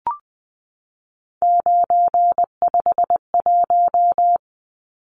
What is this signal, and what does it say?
Morse code, heard as a keyed tone, about 20 wpm, 700 Hz. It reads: 951
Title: Numbers - 3-Digit 20wpm